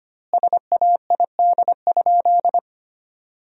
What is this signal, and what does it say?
Morse code, heard as a keyed tone, about 25 wpm, 700 Hz. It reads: SAID?
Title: 2x Word-Spacing - Sentences from Top 100 Words - 25wpm